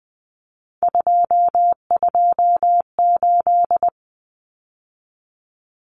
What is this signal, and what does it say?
Morse code, heard as a keyed tone, about 20 wpm, 700 Hz. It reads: 228
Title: Numbers - 3-Digit 20wpm